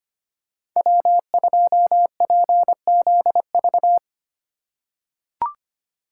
Morse code, heard as a keyed tone, about 25 wpm, 700 Hz. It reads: W2PZV
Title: USA Call Signs - 5 Characters - 2x Character Spacing - 25wpm